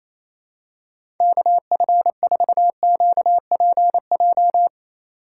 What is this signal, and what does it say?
Morse code, heard as a keyed tone, about 28 wpm, 700 Hz. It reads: KF4QPJ
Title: USA Call Signs - 6 Characters - 2x Character Spacing - 28wpm